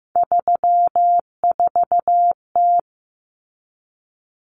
Morse code, heard as a keyed tone, about 14 wpm, 700 Hz. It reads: 34T
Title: Common Cut Numbers - 3-Digit 15wpm